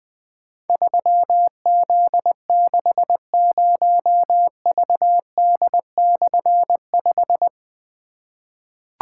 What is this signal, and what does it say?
Morse code, heard as a keyed tone, about 20 wpm, 700 Hz. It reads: 3Z60VD/5
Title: Difficult Call Signs 20wpm